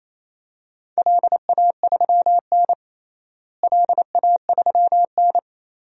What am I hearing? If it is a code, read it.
Morse code, heard as a keyed tone, about 28 wpm, 700 Hz. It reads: LA3N LA3N
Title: DXCC-100 Call Signs - 4 Characters - Repeated 2x - 28wpm